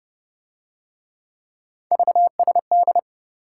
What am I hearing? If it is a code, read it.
Morse code, heard as a keyed tone, about 26 wpm, 700 Hz. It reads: VSD